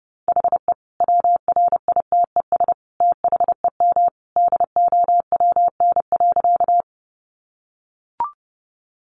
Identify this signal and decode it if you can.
Morse code, heard as a keyed tone, about 30 wpm, 700 Hz. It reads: HE WRITES THEM DOWN.
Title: Additional Easy Phrases 30wpm